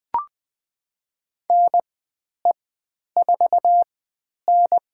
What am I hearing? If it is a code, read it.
Morse code, heard as a keyed tone, about 20 wpm, spaced as for 10 wpm, 700 Hz. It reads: NE4N
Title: 095 Lesson 16g - Letter c - Call Signs 20wpm